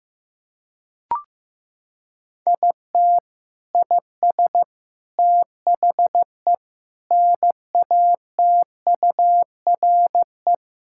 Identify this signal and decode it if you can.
Morse code, heard as a keyed tone, about 15 wpm, 700 Hz. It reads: IT IS THE NATURE
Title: Sentences from Top 1000 Words - Review Part 5 - 15wpm